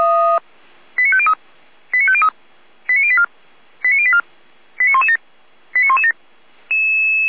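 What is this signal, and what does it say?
Five-tone selective call (ZVEI), heard as quick runs of five tones, tones of about 70 ms. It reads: 89503 89503 89084 89084 89117 89117
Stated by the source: Received by SDR, archived at audio.f85360.feuerwehr-orlamuende.org (fire brigade Orlamünde).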